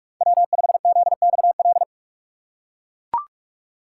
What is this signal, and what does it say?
Morse code, heard as a keyed tone, about 40 wpm, 700 Hz. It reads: W5ZXL